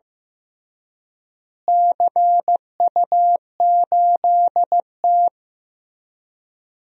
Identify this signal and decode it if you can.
Morse code, heard as a keyed tone, about 16 wpm, 700 Hz. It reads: CU8T
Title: DXCC-100 Call Signs - 4 Characters - Repeated 3x - 15wpm